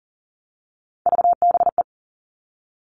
Morse code, heard as a keyed tone, about 40 wpm, 700 Hz. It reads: VBE
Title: All Cut Numbers - 3-Digit 40wpm